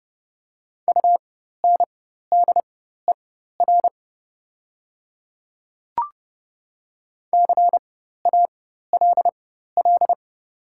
Morse code, heard as a keyed tone, about 30 wpm, spaced as for 14 wpm, 700 Hz. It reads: UNDER CALL
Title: 4x Character Spacing - Top 200 Words - 30wpm